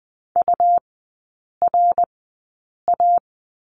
Morse code, heard as a keyed tone, about 20 wpm, spaced as for 8 wpm, 700 Hz. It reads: URA
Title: 107 Lesson 18e - Letter m - Three Characters 20wpm